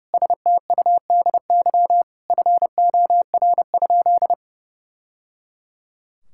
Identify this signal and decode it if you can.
Morse code, heard as a keyed tone, about 30 wpm, 700 Hz. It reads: STUDY FOR?
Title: Sentences from Top 1000 Words - Review Part 6 - 30wpm